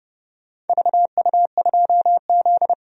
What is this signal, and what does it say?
Morse code, heard as a keyed tone, about 31 wpm, 700 Hz. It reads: VU2Z
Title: DXCC-100 Call Signs - 4 Characters - 30wpm